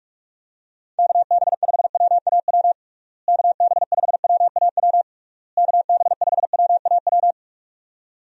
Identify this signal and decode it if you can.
Morse code, heard as a keyed tone, about 46 wpm, 700 Hz. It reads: KB5WAW KB5WAW KB5WAW
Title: USA Call Signs - 6 Characters - Repeated 3x - 45wpm